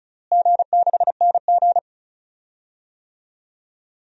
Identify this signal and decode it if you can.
Morse code, heard as a keyed tone, about 34 wpm, 700 Hz. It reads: G6NG
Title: DXCC-100 Call Signs - 4 Characters - Repeated 3x - 35wpm